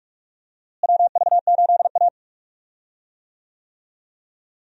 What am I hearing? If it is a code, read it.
Morse code, heard as a keyed tone, about 46 wpm, 700 Hz. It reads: WV8A
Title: USA Call Signs - 4 Characters - Repeated 3x - 45wpm